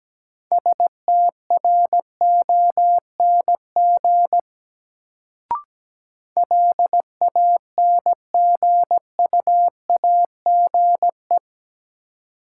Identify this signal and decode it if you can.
Morse code, heard as a keyed tone, about 17 wpm, 700 Hz. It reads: STRONG LANGUAGE
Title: Sentences from Top 500 Words - New Word Drill - 17wpm